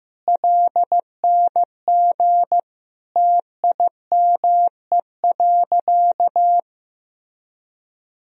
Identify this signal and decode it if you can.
Morse code, heard as a keyed tone, about 15 wpm, 700 Hz. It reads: LNG TIME.
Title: Sentences from Top 100 Words and Abbreviations 15wpm